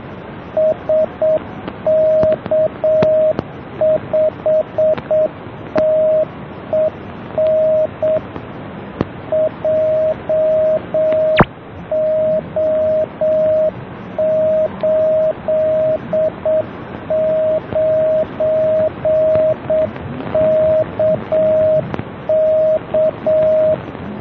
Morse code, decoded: SK5TEN JO89KK